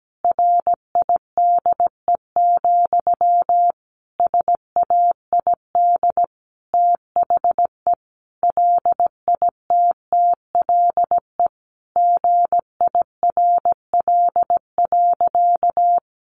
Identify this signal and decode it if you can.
Morse code, heard as a keyed tone, about 17 wpm, 700 Hz. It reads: RIDE, SAID THE LITTLE GIRL.
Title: Sentences from Top 1200 Words 17wpm